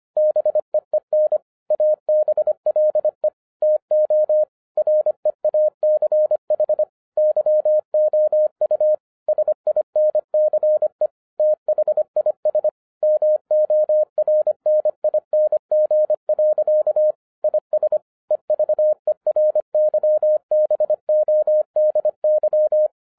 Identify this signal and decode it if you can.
Morse code, heard as a keyed tone, about 25 wpm, 610 Hz. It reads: BEEN ABLE TO REACH YOU SINCE THIS MORNING. IS EVERYBODY
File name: 2011-07-mp3 / n34 THE 3.11 2 25wpm